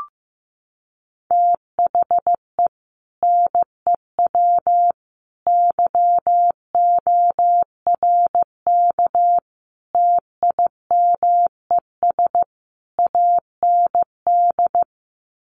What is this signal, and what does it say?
Morse code, heard as a keyed tone, about 15 wpm, 700 Hz. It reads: THE NEW YORK TIMES AND
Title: Sets of 5 Words Encore 15wpm